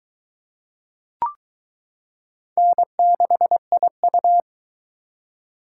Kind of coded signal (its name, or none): Morse code